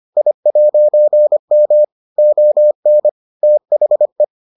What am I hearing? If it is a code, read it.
Morse code, heard as a keyed tone, about 25 wpm, 600 Hz. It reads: I'M ON THE